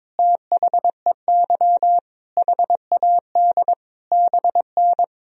Morse code, heard as a keyed tone, about 22 wpm, 700 Hz. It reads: THEY HAD BN
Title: Sentences from Top 400 Words and Abbreviations 22wpm